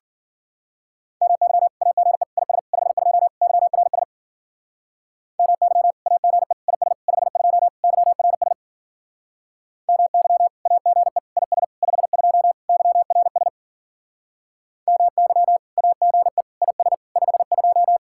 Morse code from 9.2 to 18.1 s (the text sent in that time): MY AGE IS 52 YRS MY AGE IS 52